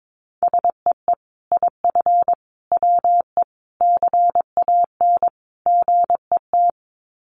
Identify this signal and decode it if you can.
Morse code, heard as a keyed tone, about 22 wpm, 700 Hz. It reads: SEE IF WE CAN GET
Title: Sets of 5 Words Encore2 22wpm